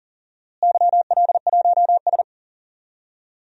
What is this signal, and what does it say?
Morse code, heard as a keyed tone, about 40 wpm, 700 Hz. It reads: YL1S
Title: DXCC-100 Call Signs - 4 Characters - Repeated 2x - 40wpm